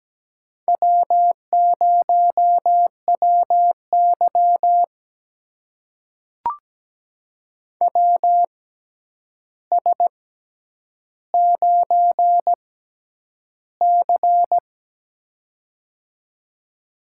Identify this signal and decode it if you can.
Morse code, heard as a keyed tone, about 17 wpm, spaced as for 9 wpm, 700 Hz. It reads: W0WY WS9C
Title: USA Call Signs - 4 Characters - 6x Character Spacing - 17wpm